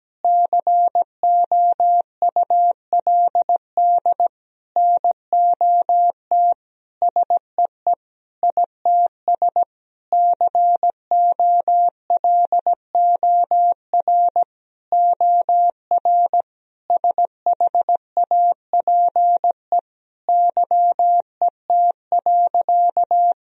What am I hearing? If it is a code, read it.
Morse code, heard as a keyed tone, about 17 wpm, 700 Hz. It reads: COULD NOT SEE ITS COLOR OR SHAPE YET.
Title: Sentences from Top 1000 Words - Review Part 3 - 17wpm